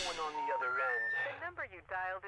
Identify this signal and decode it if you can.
Special information tone, with speech.